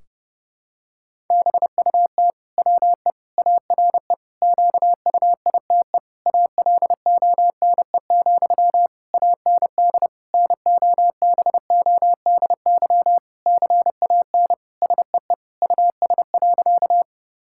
Morse code, heard as a keyed tone, about 32 wpm, 700 Hz. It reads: BUT WE ARE QUITE ALONE, AND NOBODY CAN SEE US.
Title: Sentences from Top 1600 Words 30wpm